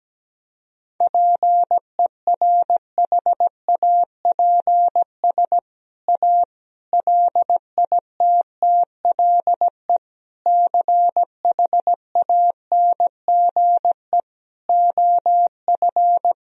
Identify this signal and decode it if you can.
Morse code, heard as a keyed tone, about 17 wpm, 700 Hz. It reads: PERHAPS A LITTLE CHANGE OF
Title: Sentences from Top 1000 Words - Review Part 1 - 17wpm